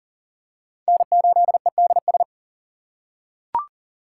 Morse code, heard as a keyed tone, about 40 wpm, 690 Hz. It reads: N8EDS
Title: USA Call Signs - 6x Character Spacing - 40wpm